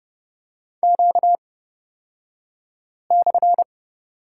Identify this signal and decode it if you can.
Morse code, heard as a keyed tone, about 31 wpm, spaced as for 5 wpm, 700 Hz. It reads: Q/
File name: Lesson 38d - Letter q - Two Characters 30wpm